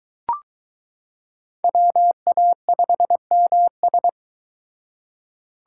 Morse code, heard as a keyed tone, about 22 wpm, 700 Hz. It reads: WA5MS